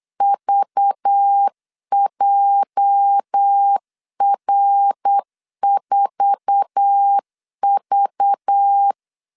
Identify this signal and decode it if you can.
Morse code, heard as a keyed tone, about 8 wpm, 800 Hz. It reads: VJR4V
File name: LEZIONI / 8 YJ45 / 8 YJ4 8wpm 3246